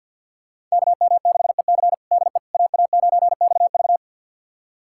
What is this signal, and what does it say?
Morse code, heard as a keyed tone, about 51 wpm, 700 Hz. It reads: KM6EC DE AA9XV
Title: USA Callsign DE Callsign - 5 Characters - 50wpm